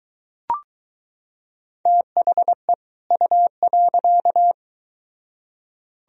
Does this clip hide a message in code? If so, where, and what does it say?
Yes: Morse code from 1.9 to 4.5 s, heard as a keyed tone, about 23 wpm, 700 Hz. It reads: THE U.